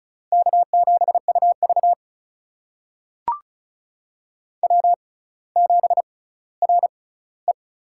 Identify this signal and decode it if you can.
Morse code, heard as a keyed tone, about 36 wpm, spaced as for 18 wpm, 700 Hz. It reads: K7UV W7RE
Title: USA Call Signs - 4 Characters - 6x Character Spacing - 35wpm